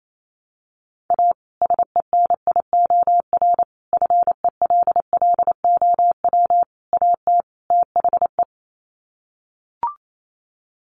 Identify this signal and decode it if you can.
Morse code, heard as a keyed tone, about 28 wpm, 700 Hz. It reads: A SENIOR FELLOW AT THE
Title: Sets of 5 Words Encore 28wpm